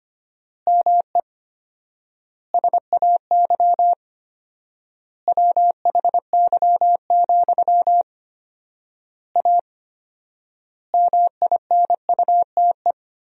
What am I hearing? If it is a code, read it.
Morse code, heard as a keyed tone, about 25 wpm, 700 Hz. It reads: ME SAY WHY, A MINUTE
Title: 4x Word-Spacing - Sentences from Top 500 Words - 25wpm